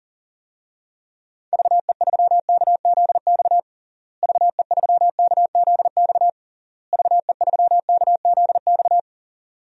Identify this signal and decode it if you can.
Morse code, heard as a keyed tone, about 40 wpm, 700 Hz. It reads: VE3KZX VE3KZX VE3KZX
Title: DXCC-100 Call Signs - 6 Characters - Repeated 3x - 40wpm